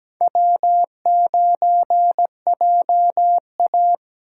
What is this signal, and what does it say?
Morse code, heard as a keyed tone, about 17 wpm, 700 Hz. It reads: W9JA
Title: USA Call Signs - 4 Characters - 4x Character Spacing - 17wpm